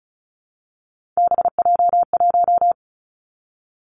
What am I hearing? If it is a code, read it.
Morse code, heard as a keyed tone, about 34 wpm, 700 Hz. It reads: BJ1
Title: Three Challenging Letter-Number Combinations 35wpm